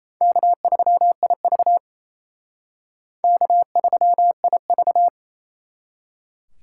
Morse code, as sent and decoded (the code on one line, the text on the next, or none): -.- ...-- .. ...- / -.- ...-- .. ...-
K3IV K3IV